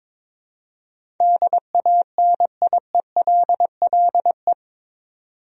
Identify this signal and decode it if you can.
Morse code, heard as a keyed tone, about 22 wpm, 700 Hz. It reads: DANIELLE
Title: USA Common Names - 22wpm